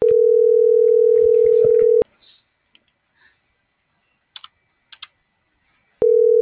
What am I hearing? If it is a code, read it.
Ringback tone (North America), 440 + 480 Hz.